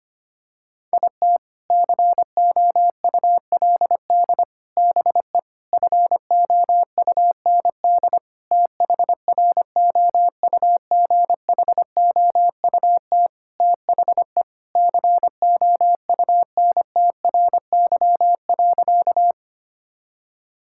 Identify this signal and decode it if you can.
Morse code, heard as a keyed tone, about 25 wpm, 700 Hz. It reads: IT COULD BE FOUND THROUGHOUT THE COUNTRY.
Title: Sentences from Top 1000 Words - Review Part 5 - 25wpm